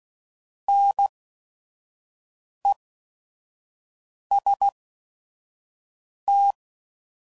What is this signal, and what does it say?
Morse code, heard as a keyed tone, about 16 wpm, spaced as for 5 wpm, 800 Hz. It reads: NEST
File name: Prefixes / By First Letter / Ns